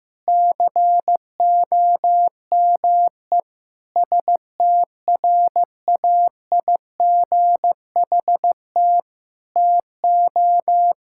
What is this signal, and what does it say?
Morse code, heard as a keyed tone, about 15 wpm, 700 Hz. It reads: COME STRAIGHT TO
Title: Sentences from Top 1000 Words - Review Part 6 - 15wpm